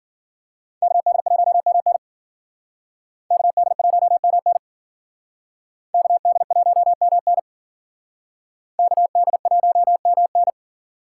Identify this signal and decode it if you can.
Morse code, heard as a keyed tone, about 51 wpm, 700 Hz. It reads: KD1MN KD1MN KD1MN KD1MN